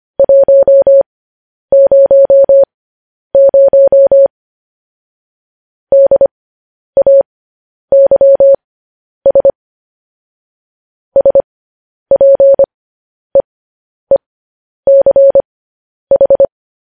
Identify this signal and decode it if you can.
Morse code, heard as a keyed tone, about 25 wpm, spaced as for 10 wpm, 550 Hz. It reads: 100 DAYS SPEECH
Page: news headlines in morse code at 10 wpm